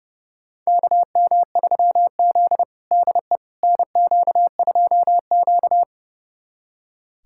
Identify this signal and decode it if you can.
Morse code, heard as a keyed tone, about 32 wpm, 700 Hz. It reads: KM3Z DE NQ2Q
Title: USA Callsign DE Callsign - 4 Characters - 30wpm